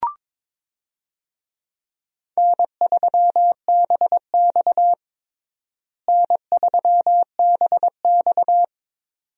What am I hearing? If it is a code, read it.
Morse code, heard as a keyed tone, about 22 wpm, 700 Hz. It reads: N3BX N3BX
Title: USA Call Signs - 4 Characters - Repeated 2x - 22wpm